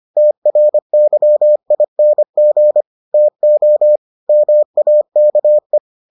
Morse code, heard as a keyed tone, about 25 wpm, 600 Hz. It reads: TRYING TO MAKE